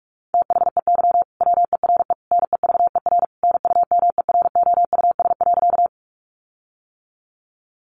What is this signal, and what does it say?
Morse code, heard as a keyed tone, about 44 wpm, 700 Hz. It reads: THEY WERE NEVER NUMEROUS.
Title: Sentences from Top 1400 Words 45wpm